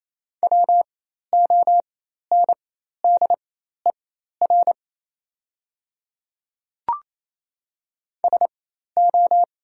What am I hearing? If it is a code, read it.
Morse code, heard as a keyed tone, about 27 wpm, spaced as for 13 wpm, 690 Hz. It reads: WONDER SO
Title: 4x Character Spacing - Top 500 Words - 28wpm